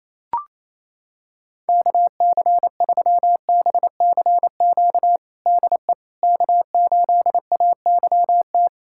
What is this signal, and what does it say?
Morse code, heard as a keyed tone, about 28 wpm, 700 Hz. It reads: KC3BCQ DE K8AYT
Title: USA Callsign DE Callsign 28wpm